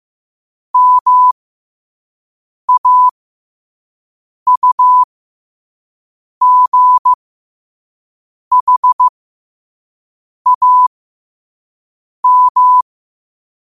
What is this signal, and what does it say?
Morse code, heard as a keyed tone, about 15 wpm, spaced as for 5 wpm, 1000 Hz. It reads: MAUGHAM
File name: qotdfiles / 5wpm